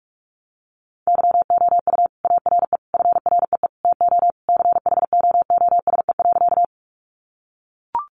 Morse code, heard as a keyed tone, about 44 wpm, 700 Hz. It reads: YOU ARE FREE TO CHOOSE.
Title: Sentences from Top 700 Words 45wpm